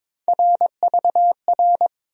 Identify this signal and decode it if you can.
Morse code, heard as a keyed tone, about 22 wpm, 700 Hz. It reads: RVR
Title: USA Call Sign Postfixes - 3 Letters 22wpm